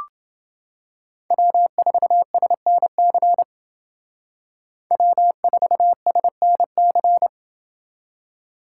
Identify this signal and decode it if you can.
Morse code, heard as a keyed tone, about 28 wpm, 700 Hz. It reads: W4SNC W4SNC